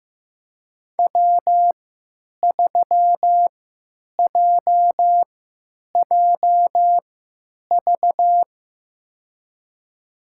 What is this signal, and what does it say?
Morse code, heard as a keyed tone, about 15 wpm, spaced as for 9 wpm, 700 Hz. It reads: W3JJV
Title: USA Call Signs - 5 Characters - 3x Character Spacing - 15wpm